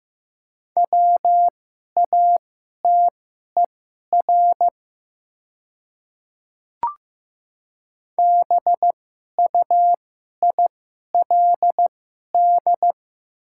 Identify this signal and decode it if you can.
Morse code, heard as a keyed tone, about 15 wpm, spaced as for 11 wpm, 700 Hz. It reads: WATER BUILD
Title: 2x Character Spacing - Top 200 Words - 15wpm